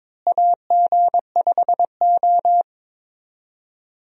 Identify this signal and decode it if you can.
Morse code, heard as a keyed tone, about 22 wpm, 700 Hz. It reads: AG5O
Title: USA Call Signs - 4 Characters - Repeated 2x - 22wpm